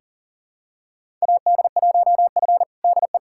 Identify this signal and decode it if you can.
Morse code, heard as a keyed tone, about 40 wpm, 700 Hz. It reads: AD1F DE
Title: USA Callsign DE Callsign - 4 Characters - 40wpm